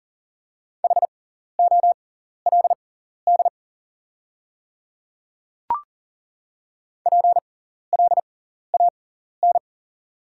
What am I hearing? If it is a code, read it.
Morse code, heard as a keyed tone, about 40 wpm, spaced as for 14 wpm, 690 Hz. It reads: HOLD PLAN
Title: 6x Character Spacing - Top 500 Words - 40wpm